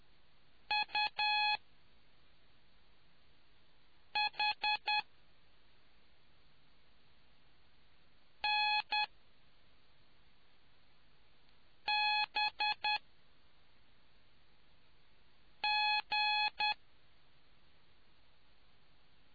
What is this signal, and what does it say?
Morse code, heard as a keyed tone, about 10 wpm, spaced as for 3 wpm, 840 Hz. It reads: UHNBG